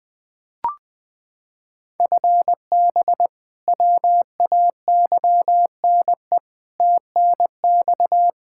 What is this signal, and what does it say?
Morse code, heard as a keyed tone, about 20 wpm, 700 Hz. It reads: FB WAYNE TNX
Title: Practice QSO - Exchange 3 - More Info - 20wpm